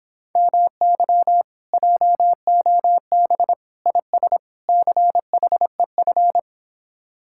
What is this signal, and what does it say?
Morse code, heard as a keyed tone, about 26 wpm, 700 Hz. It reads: MY JOB IS CHEF